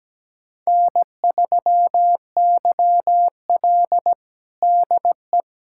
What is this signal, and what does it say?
Morse code, heard as a keyed tone, about 17 wpm, 700 Hz. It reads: N3YL DE
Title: USA Callsign DE Callsign - 4 Characters - 17wpm